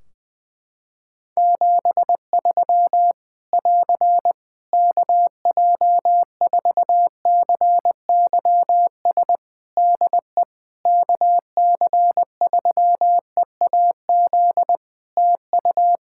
Morse code, heard as a keyed tone, about 20 wpm, 700 Hz. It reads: 73 <AR> KJ4CYS DE KC3EAZ TU
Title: Practice QSO - Exchange 4 - Ending - 20wpm